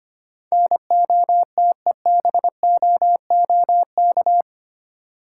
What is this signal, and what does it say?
Morse code, heard as a keyed tone, about 25 wpm, 690 Hz. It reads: NOTEBOOK
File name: practice / 25wpm / Easy Words with Hints - 25wpm